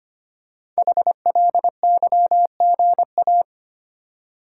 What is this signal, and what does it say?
Morse code, heard as a keyed tone, about 25 wpm, 700 Hz. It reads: HLYGA
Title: Code Groups - 5 Letters - 25wpm